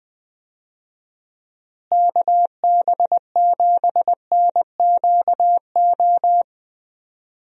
Morse code, heard as a keyed tone, about 21 wpm, 700 Hz. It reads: KB7NQO
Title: USA Call Signs - 6 Characters - Repeated 3x - 20wpm